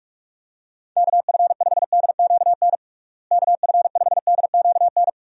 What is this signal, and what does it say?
Morse code, heard as a keyed tone, about 45 wpm, 700 Hz. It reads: KF5DQN KF5DQN
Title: USA Call Signs - 6 Characters - Repeated 2x - 45wpm